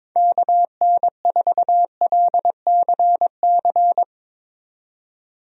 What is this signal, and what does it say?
Morse code, heard as a keyed tone, about 22 wpm, 700 Hz. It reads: KN4LCC
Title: USA Call Signs - 6 Characters - 22wpm